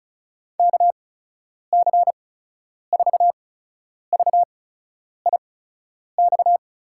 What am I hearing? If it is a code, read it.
Morse code, heard as a keyed tone, about 36 wpm, spaced as for 10 wpm, 700 Hz. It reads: KC4VIX
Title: USA Call Signs - 8x Character Spacing - 35wpm